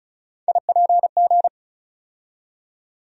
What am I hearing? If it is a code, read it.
Morse code, heard as a keyed tone, about 34 wpm, 700 Hz. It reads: IPG
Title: USA Call Sign Postfixes - 3 Letters 35wpm